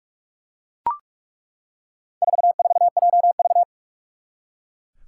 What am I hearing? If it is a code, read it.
Morse code, heard as a keyed tone, about 41 wpm, 700 Hz. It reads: 44JV
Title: Four Challenging Letter-Number Combinations 45wpm